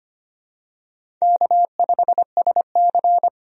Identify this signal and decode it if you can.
Morse code, heard as a keyed tone, about 25 wpm, 700 Hz. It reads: K5SC
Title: USA Call Signs - 4 Characters - 25wpm